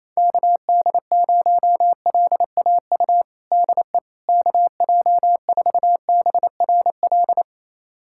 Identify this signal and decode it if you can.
Morse code, heard as a keyed tone, about 28 wpm, 700 Hz. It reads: KD0LAU DE KJ4BRL
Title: USA Callsign DE Callsign 28wpm